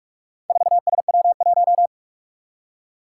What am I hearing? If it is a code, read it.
Morse code, heard as a keyed tone, about 46 wpm, 700 Hz. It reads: 4SW1
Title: Four Challenging Letter-Number Combinations 45wpm